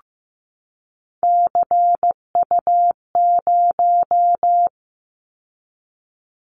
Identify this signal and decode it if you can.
Morse code, heard as a keyed tone, about 15 wpm, 700 Hz. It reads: CU0
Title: Callsign Prefixes for Top 50 DXCC - 15wpm